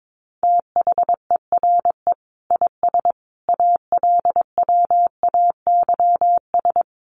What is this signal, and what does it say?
Morse code, heard as a keyed tone, about 22 wpm, 700 Hz. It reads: THERE IS ALWAYS
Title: Sentences from Top 1000 Words 22wpm